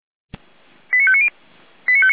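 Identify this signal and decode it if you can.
Five-tone selective call (ZVEI), heard as quick runs of five tones, tones of about 70 ms. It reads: 89590
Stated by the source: ZVEI five-tone selective call received on an SDR.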